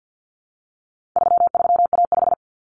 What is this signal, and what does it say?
Morse code, heard as a keyed tone, about 48 wpm, 700 Hz. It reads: 33A5
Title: Common Cut Numbers - 4-Digit 50wpm